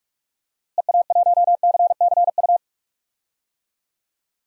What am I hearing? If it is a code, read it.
Morse code, heard as a keyed tone, about 41 wpm, 700 Hz. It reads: EA1CCU
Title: DXCC-100 Call Signs - 6 Characters - Repeated 3x - 45wpm